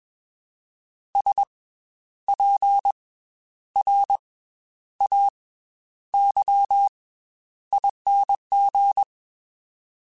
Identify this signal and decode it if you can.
Morse code, heard as a keyed tone, about 21 wpm, 800 Hz. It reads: S P R A Y ING
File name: Special INGs / Special INGs-mp3s / Slow ing @20